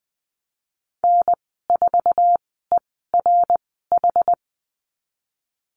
Morse code, heard as a keyed tone, about 20 wpm, spaced as for 15 wpm, 700 Hz. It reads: N4ERH
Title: 069 Lesson 12g - Letter d - Call Signs 20wpm